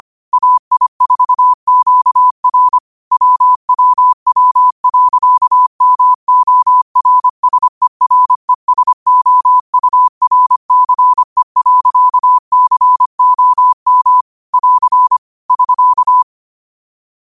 Morse code, heard as a keyed tone, about 25 wpm, 1000 Hz. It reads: AI4QR WWW.MORSERESOURCE.COM <AR> <SK>
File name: qotdfiles / 25wpm